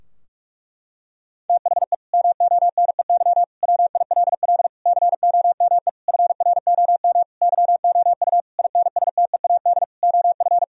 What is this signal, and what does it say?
Morse code, heard as a keyed tone, about 46 wpm, 700 Hz. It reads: THE MONEY WILL COME FROM YOU INSTEAD OF
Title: Sentences from Top 1000 Words - Review Part 4 - 45wpm